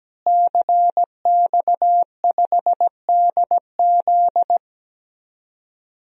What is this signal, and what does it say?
Morse code, heard as a keyed tone, about 17 wpm, 700 Hz. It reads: CX5DZ
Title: DXCC-100 Call Signs - 5 Characters - Repeated 3x - 17wpm